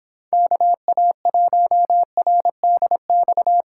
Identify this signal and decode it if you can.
Morse code, heard as a keyed tone, about 26 wpm, 700 Hz. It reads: KA1RDX